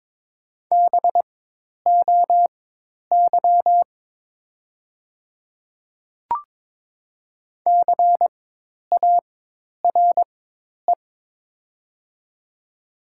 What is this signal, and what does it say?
Morse code, heard as a keyed tone, about 22 wpm, spaced as for 10 wpm, 700 Hz. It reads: BOY CARE
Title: 4x Character Spacing - Top 300 Words - 22wpm